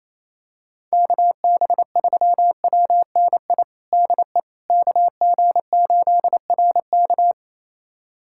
Morse code, heard as a keyed tone, about 28 wpm, 700 Hz. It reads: KB3WNI DE KG8RK
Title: USA Callsign DE Callsign 28wpm